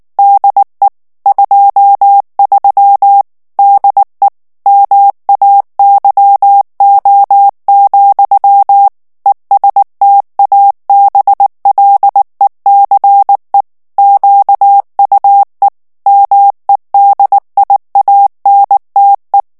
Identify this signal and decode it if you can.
Morse code, heard as a keyed tone, about 19 wpm, 800 Hz. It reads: DE 23 DE MAYO, ESTABLECE QUE MEDIANTE
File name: morse mp3 / ESPAGNOL TEXTE EN CLAIR 2 VITESSE 18 MOTS MINUTES